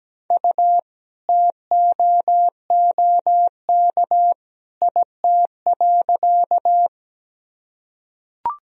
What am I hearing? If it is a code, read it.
Morse code, heard as a keyed tone, about 17 wpm, 700 Hz. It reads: U TOOK IT.
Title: Sentences from Top 300 Words and Abbreviations 17wpm